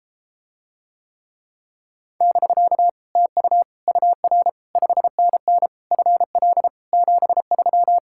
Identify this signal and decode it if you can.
Morse code, heard as a keyed tone, about 33 wpm, 700 Hz. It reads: <BK> TU UR 5NN FL 73